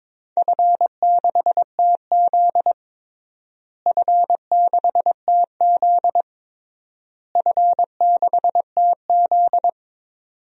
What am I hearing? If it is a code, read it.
Morse code, heard as a keyed tone, about 22 wpm, 700 Hz. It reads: F6TZ F6TZ F6TZ
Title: DXCC-100 Call Signs - 4 Characters - Repeated 3x - 22wpm